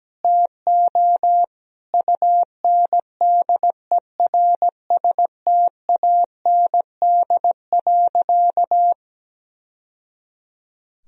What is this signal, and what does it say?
Morse code, heard as a keyed tone, about 17 wpm, 700 Hz. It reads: TO UNDERSTAND.
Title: Sentences from Top 1000 Words - Review Part 3 - 17wpm